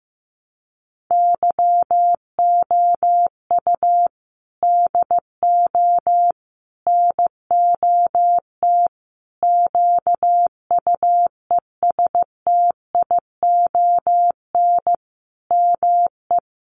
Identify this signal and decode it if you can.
Morse code, heard as a keyed tone, about 15 wpm, 700 Hz. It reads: YOU DO NOT QUESTION ME
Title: Sentences from Top 1100 Words 15wpm